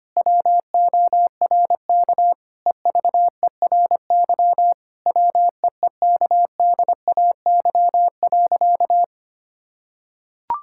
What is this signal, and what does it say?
Morse code, heard as a keyed tone, about 25 wpm, 700 Hz. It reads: WORK EVERY WEEKDAY.
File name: practice / 25wpm / Difficult Sentences with Hints - 25wpm